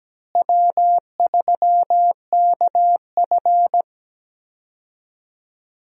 Morse code, heard as a keyed tone, about 17 wpm, 700 Hz. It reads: W3KF
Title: USA Call Signs - 4 Characters - Repeated 3x - 17wpm